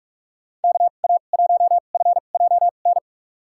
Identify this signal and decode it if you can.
Morse code, heard as a keyed tone, about 44 wpm, spaced as for 32 wpm, 700 Hz. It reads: KA1FJN
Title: USA Call Signs - 6 Characters - 2x Character Spacing - 45wpm